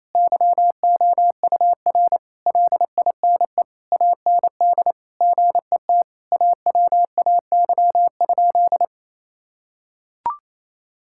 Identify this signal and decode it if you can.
Morse code, heard as a keyed tone, about 28 wpm, 700 Hz. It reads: YOUR LINE AND GET AWAY?
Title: Sentences from Top 1000 Words - Review Part 6 - 28wpm